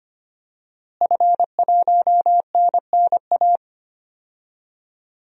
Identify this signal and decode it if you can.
Morse code, heard as a keyed tone, about 25 wpm, 700 Hz. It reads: F1NNA
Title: DXCC-100 Call Signs - 5 Characters - Repeated 3x - 25wpm